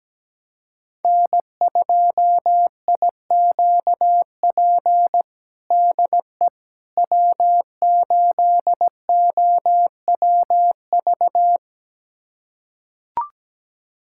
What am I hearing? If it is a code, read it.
Morse code, heard as a keyed tone, about 17 wpm, 700 Hz. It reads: N2IQP DE W8OWV
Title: USA Callsign DE Callsign - 5 Characters - 17wpm